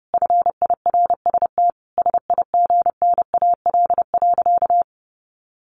Morse code, heard as a keyed tone, about 30 wpm, 700 Hz. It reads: FIRST SIGNAL.
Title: Sentences from Top 1400 Words 30wpm